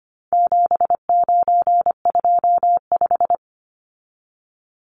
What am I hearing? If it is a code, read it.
Morse code, heard as a keyed tone, about 25 wpm, 700 Hz. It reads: 7925
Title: Numbers - 4-Digit 25wpm